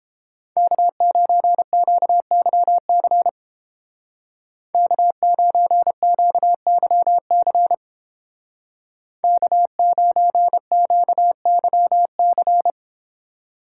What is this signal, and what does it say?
Morse code, heard as a keyed tone, about 30 wpm, 700 Hz. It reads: K9QYC K9QYC K9QYC